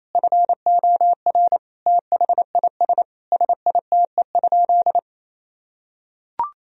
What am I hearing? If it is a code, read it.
Morse code, heard as a keyed tone, about 28 wpm, 700 Hz. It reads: FOR THIS SITE?
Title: Sentences from Top 1000 Words - Review Part 4 - 28wpm